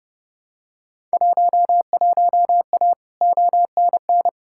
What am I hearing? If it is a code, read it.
Morse code, heard as a keyed tone, about 31 wpm, 700 Hz. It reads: 11A ONN
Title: Field Day Exchanges Encore - 30wpm